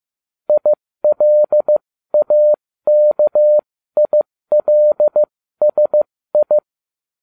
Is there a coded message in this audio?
Yes: Morse code from 0.5 to 6.6 s, heard as a keyed tone, about 15 wpm, spaced as for 13 wpm, 600 Hz. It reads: ILAKILSI